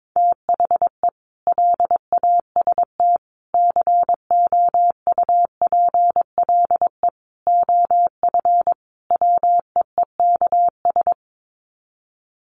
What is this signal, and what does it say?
Morse code, heard as a keyed tone, about 22 wpm, 700 Hz. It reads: THE LAST COUPLE OF WEEKS
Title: Sets of 5 Words Encore2 22wpm